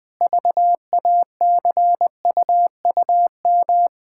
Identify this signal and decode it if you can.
Morse code, heard as a keyed tone, about 20 wpm, 700 Hz. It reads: VACUUM
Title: Pangram Challenge v2 - 20wpm